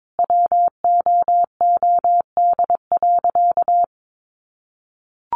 Morse code, heard as a keyed tone, about 22 wpm, 700 Hz. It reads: WOOD.
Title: Sentences from Top 1200 Words 22wpm